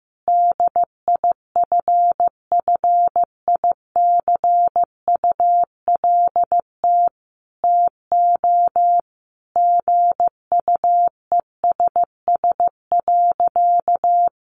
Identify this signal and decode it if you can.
Morse code, heard as a keyed tone, about 14 wpm, 700 Hz. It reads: DIFFICULT TO GUESS.
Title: Sentences from Top 1500 Words 15wpm